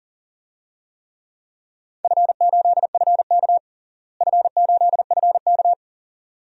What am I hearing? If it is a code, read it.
Morse code, heard as a keyed tone, about 37 wpm, 700 Hz. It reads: F8FK F8FK
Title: DXCC-100 Call Signs - 4 Characters - Repeated 2x - 40wpm